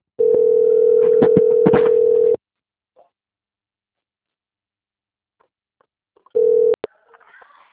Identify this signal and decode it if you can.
Ringback tone (North America), 440 + 480 Hz.